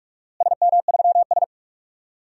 Morse code, heard as a keyed tone, about 46 wpm, 700 Hz. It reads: SM3S
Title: DXCC-100 Call Signs - 4 Characters - Repeated 3x - 45wpm